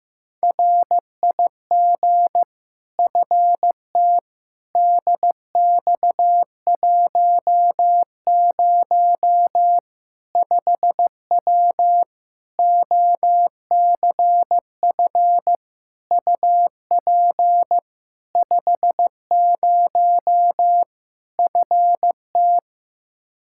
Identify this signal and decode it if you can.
Morse code, heard as a keyed tone, about 15 wpm, 700 Hz. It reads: RIG FT DX10 5W OCF UP 50 FT
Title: Practice QSO - Exchange 2 - Info - 15wpm